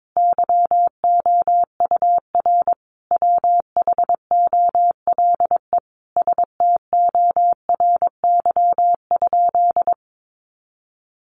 Morse code, heard as a keyed tone, about 22 wpm, 700 Hz. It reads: YOUR WHOLE STORY?
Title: Sentences from Top 400 Words 22wpm